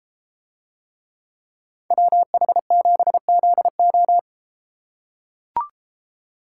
Morse code, heard as a keyed tone, about 32 wpm, 700 Hz. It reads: WH7ZO